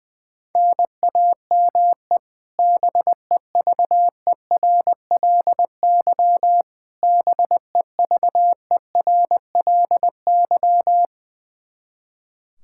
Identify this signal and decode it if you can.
Morse code, heard as a keyed tone, about 20 wpm, 700 Hz. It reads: NAME BEVERLY BEVERLY
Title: Practice QSO - Exchange 1 - Beginning - 20wpm